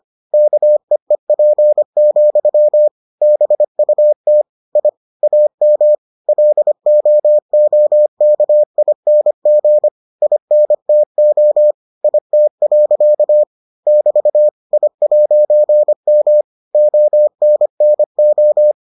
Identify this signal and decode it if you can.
Morse code, heard as a keyed tone, about 25 wpm, 600 Hz. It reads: KEEP, BUT I AM LOOKING INTO IT. <BT> I'M ONNO